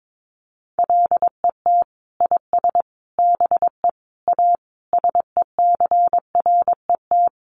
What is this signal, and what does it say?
Morse code, heard as a keyed tone, about 22 wpm, 700 Hz. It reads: LET IS BE A SECRET
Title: Sentences from Top 1200 Words 22wpm